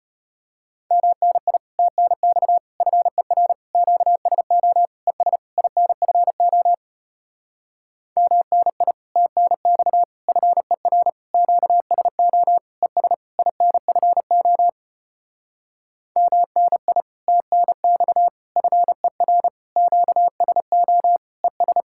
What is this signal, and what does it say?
Morse code, heard as a keyed tone, about 33 wpm, 700 Hz. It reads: MNI TNX FER QSO ES INFO MNI TNX FER QSO ES INFO MNI TNX FER QSO ES